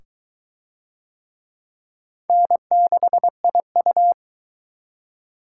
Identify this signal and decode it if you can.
Morse code, heard as a keyed tone, about 20 wpm, 700 Hz. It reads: N6IU